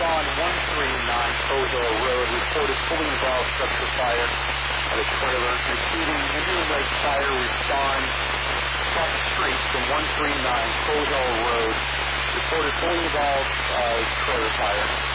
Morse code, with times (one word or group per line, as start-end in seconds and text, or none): none